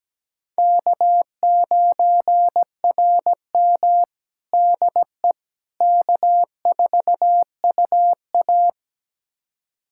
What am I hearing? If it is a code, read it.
Morse code, heard as a keyed tone, about 17 wpm, 700 Hz. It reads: K9RM DE K4UA
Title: USA Callsign DE Callsign - 4 Characters - 17wpm